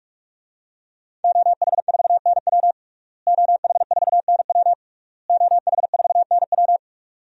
Morse code, heard as a keyed tone, about 42 wpm, 700 Hz. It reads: OH4NW OH4NW OH4NW
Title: DXCC-100 Call Signs - 5 Characters - Repeated 3x - 45wpm